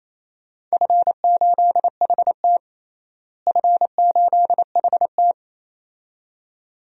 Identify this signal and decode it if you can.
Morse code, heard as a keyed tone, about 28 wpm, 700 Hz. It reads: F8HT F8HT
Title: DXCC-100 Call Signs - 4 Characters - Repeated 2x - 28wpm